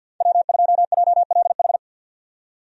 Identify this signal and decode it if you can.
Morse code, heard as a keyed tone, about 47 wpm, 700 Hz. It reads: W2JLH
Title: USA Call Signs - 5 Characters - 50wpm